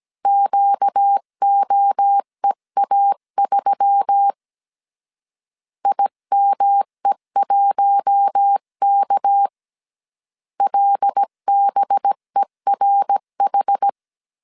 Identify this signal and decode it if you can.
Morse code, heard as a keyed tone, about 17 wpm, 800 Hz. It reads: QOEA3 IME1K LBERH
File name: CW / DET / 1 6 1500C S C 16wpm 2733